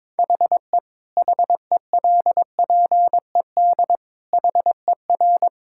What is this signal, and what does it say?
Morse code, heard as a keyed tone, about 22 wpm, 700 Hz. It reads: HE HELPED HER
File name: practice / 22wpm / Sentences with 8-Word Phrases - 22wpm